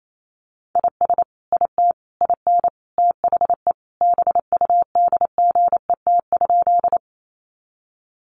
Morse code, heard as a keyed tone, about 28 wpm, 700 Hz. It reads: IS IT IN THE BUDGET?
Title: Sentences from Top 1200 Words 28wpm